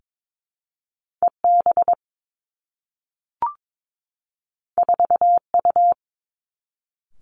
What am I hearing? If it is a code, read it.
Morse code, heard as a keyed tone, about 22 wpm, 700 Hz. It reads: E B 4 U
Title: Two Challenging Letter-Number Combinations 22wpm